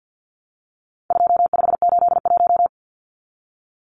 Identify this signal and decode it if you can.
Morse code, heard as a keyed tone, about 50 wpm, 700 Hz. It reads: 2581
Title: Numbers - 4-Digit 50wpm